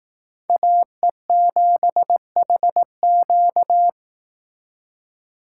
Morse code, heard as a keyed tone, about 18 wpm, 700 Hz. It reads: AE7HQ
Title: Speed Racing - USA 2x2 Call Signs 15wpm